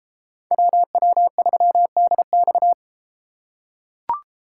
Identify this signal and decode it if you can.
Morse code, heard as a keyed tone, about 30 wpm, 700 Hz. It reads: WW3DX